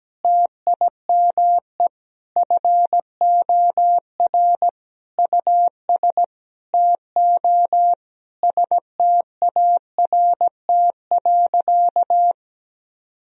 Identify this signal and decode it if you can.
Morse code, heard as a keyed tone, about 17 wpm, 700 Hz. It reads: TIME FOR US TO START.
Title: Sentences from Top 1000 Words - Review Part 4 - 17wpm